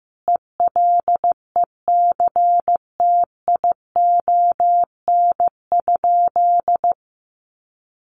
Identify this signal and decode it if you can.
Morse code, heard as a keyed tone, about 14 wpm, 700 Hz. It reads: ELECTION?
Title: Sentences from Top 1500 Words 15wpm